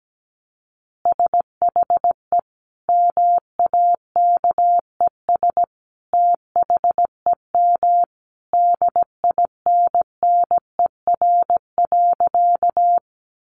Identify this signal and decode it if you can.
Morse code, heard as a keyed tone, about 17 wpm, 700 Hz. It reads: SHE MAKES THEM DINNER.
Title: Easy Phrases 17wpm